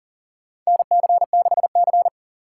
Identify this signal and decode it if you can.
Morse code, heard as a keyed tone, about 40 wpm, 700 Hz. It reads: NC6C
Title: USA Call Signs - 4 Characters - 3x Character Spacing - 40wpm